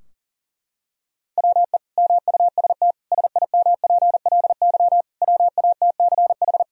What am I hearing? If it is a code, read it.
Morse code, heard as a keyed tone, about 37 wpm, 700 Hz. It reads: WE MUST SIMPLY WATCH
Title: Sentences from Top 1000 Words - Review Part 4 - 40wpm